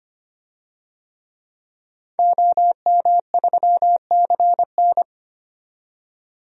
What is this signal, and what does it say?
Morse code, heard as a keyed tone, about 25 wpm, 700 Hz. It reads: OM3CN
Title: DXCC-100 Call Signs - 5 Characters - 25wpm